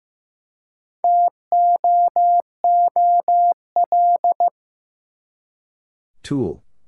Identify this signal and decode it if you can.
Morse code, heard as a keyed tone, about 15 wpm, 700 Hz. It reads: TOOL
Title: Four Letter Words - 15wpm